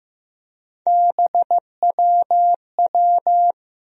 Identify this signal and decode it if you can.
Morse code, heard as a keyed tone, about 15 wpm, 700 Hz. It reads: BWW
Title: USA Call Sign Postfixes - 3 Letters 15wpm